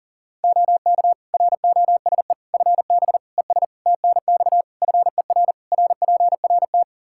Morse code, heard as a keyed tone, about 40 wpm, 700 Hz. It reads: OK ROSE FB ES TNX FER RPRT
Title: Practice QSO - Exchange 2 - Info - 40wpm